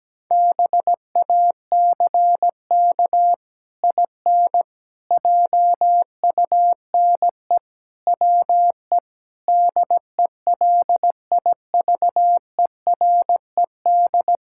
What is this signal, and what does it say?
Morse code, heard as a keyed tone, about 17 wpm, 700 Hz. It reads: BACK IN JUNE WE DELIVERED
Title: Pangram Challenge v2 - 17wpm